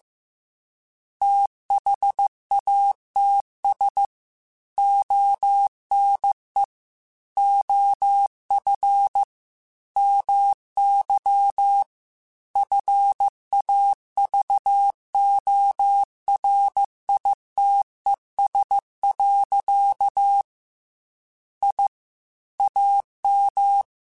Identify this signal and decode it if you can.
Morse code, heard as a keyed tone, about 15 wpm, 800 Hz. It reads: THATS ONE OF MY FAVORITES. I AM
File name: QSOs / QSOs-mp3s / WD0FBT 1@13